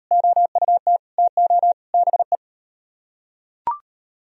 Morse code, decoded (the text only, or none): OUT TO BE